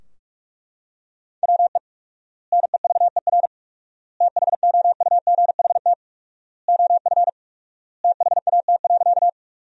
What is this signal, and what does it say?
Morse code, heard as a keyed tone, about 41 wpm, 700 Hz. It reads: WE NEVER THOUGHT OF THAT.
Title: 4x Word-Spacing - Sentences from Top 300 Words - 45wpm